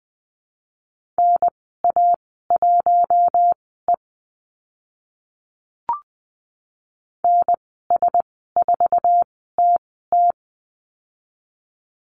Morse code, heard as a keyed tone, about 20 wpm, spaced as for 14 wpm, 700 Hz. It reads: NA1E NS4TT
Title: 055 Lesson 10g - Letter r - Call Signs 20wpm